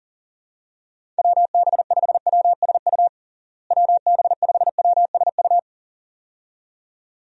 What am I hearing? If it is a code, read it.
Morse code, heard as a keyed tone, about 37 wpm, 700 Hz. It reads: WB5WSU WB5WSU
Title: USA Call Signs - 6 Characters - Repeated 2x - 40wpm